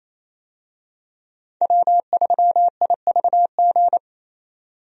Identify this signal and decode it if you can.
Morse code, heard as a keyed tone, about 28 wpm, 700 Hz. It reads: W3IVG
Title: USA Call Signs - 5 Characters - 28wpm